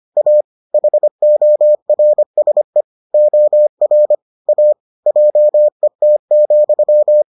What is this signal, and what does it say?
Morse code, heard as a keyed tone, about 25 wpm, 600 Hz. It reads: A HORSE OR A JET,